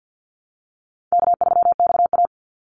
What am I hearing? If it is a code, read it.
Morse code, heard as a keyed tone, about 50 wpm, 700 Hz. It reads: K3XA
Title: Call Signs Repeated 3x - 50wpm